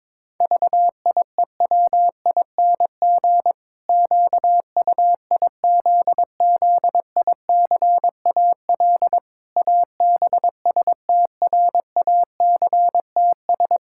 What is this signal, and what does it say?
Morse code, heard as a keyed tone, about 22 wpm, 700 Hz. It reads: VIEWING QUIZZICAL ABSTRACTS
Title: Pangram Challenge v2 - 22wpm